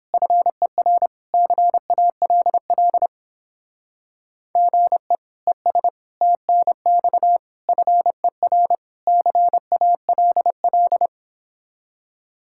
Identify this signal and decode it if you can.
Morse code, heard as a keyed tone, about 29 wpm, 700 Hz. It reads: FER CALL GE ES TNX FER CALL